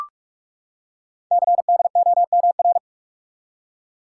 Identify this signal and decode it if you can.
Morse code, heard as a keyed tone, about 44 wpm, 700 Hz. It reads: CDOMR
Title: Code Groups - 5 Letters - 45wpm